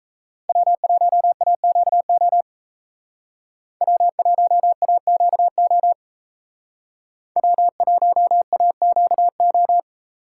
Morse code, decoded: W1AQO W1AQO W1AQO

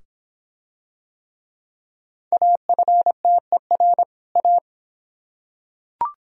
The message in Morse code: AFTER A

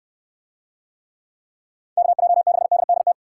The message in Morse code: KJ6DDT

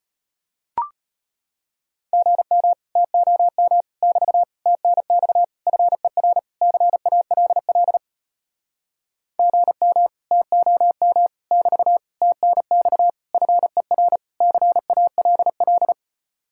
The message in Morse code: GM TOM <BT> TNX FER CALL GM TOM <BT> TNX FER CALL

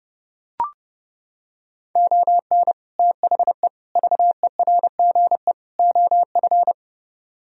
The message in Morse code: ON THE VERGE OF

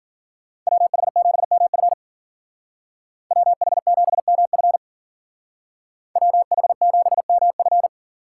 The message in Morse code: WH7MF WH7MF WH7MF

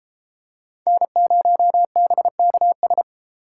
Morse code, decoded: N0BKS

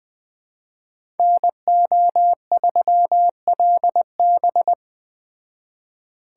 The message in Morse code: NO3LB

